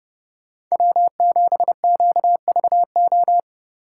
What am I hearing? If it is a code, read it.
Morse code, heard as a keyed tone, about 36 wpm, 700 Hz. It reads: W7QVO
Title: Speed Racing - USA 1x3 Call Signs 30wpm